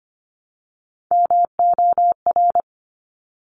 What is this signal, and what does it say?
Morse code, heard as a keyed tone, about 25 wpm, 700 Hz. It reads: MOR
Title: Top Three Letter Combinations in Words 25wpm